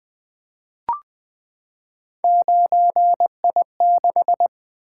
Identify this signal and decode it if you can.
Morse code, heard as a keyed tone, about 20 wpm, 700 Hz. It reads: 9I6
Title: Three Letter-Number Combinations - 20wpm